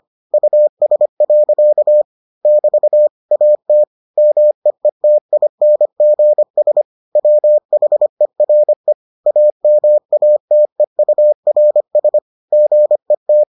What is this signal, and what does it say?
Morse code, heard as a keyed tone, about 25 wpm, 600 Hz. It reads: US. <BT> AT MEETINGS WHERE AMATEURS GET